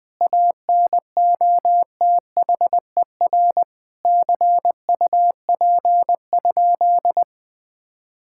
Morse code, heard as a keyed tone, about 19 wpm, 700 Hz. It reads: ANOTHER CUP?
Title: Sentences from Top 1600 Words 20wpm